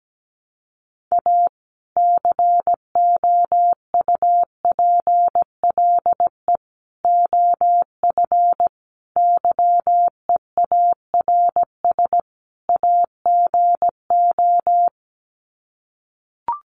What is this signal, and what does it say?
Morse code, heard as a keyed tone, about 17 wpm, 700 Hz. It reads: A COUPLE OF YEARS AGO
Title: Sets of 5 Words 17wpm